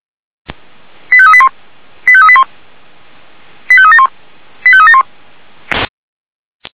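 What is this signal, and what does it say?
Five-tone selective call (ZVEI), heard as quick runs of five tones, tones of about 70 ms. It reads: 85381 85381 85371 85371
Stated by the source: ZVEI five-tone selective call received on SDR.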